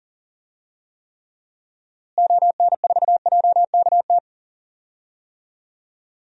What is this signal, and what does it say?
Morse code, heard as a keyed tone, about 38 wpm, 700 Hz. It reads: ON4JKT
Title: DXCC-100 Call Signs - 6 Characters - Repeated 2x - 40wpm